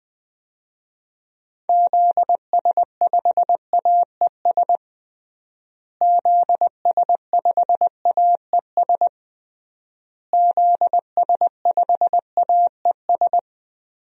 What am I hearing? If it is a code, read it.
Morse code, heard as a keyed tone, about 21 wpm, 700 Hz. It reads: ZS5AES ZS5AES ZS5AES
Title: DXCC-100 Call Signs - 6 Characters - Repeated 3x - 20wpm